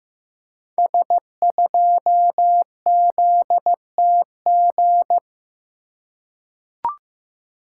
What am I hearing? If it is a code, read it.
Morse code, heard as a keyed tone, about 15 wpm, 700 Hz. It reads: S2ZTG
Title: Code Groups - 5 Letter:Numbers - 15wpm